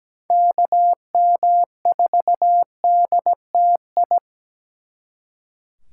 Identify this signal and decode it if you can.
Morse code, heard as a keyed tone, about 20 wpm, 700 Hz. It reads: KM4DTI
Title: Speed Racing - USA 2x3 Call Signs 17wpm